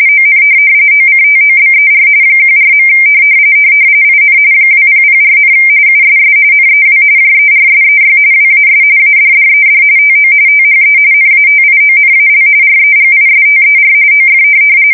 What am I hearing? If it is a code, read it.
RTTY, about 45.45 baud. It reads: ON OCTOBER 14. IN 2004, USS NEW JERSEY WAS ADDED TO THE NATIONAL REGISTER OF
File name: fsk170 / USS-New-Jersey